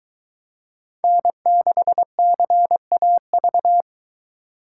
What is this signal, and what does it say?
Morse code, heard as a keyed tone, about 22 wpm, 700 Hz. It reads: N6CAV